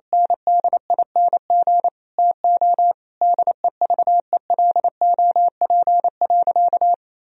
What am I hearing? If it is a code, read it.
Morse code, heard as a keyed tone, about 28 wpm, 700 Hz. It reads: NDING TO DEVELOP.
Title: Sentences from Top 1000 Words - Review Part 3 - 28wpm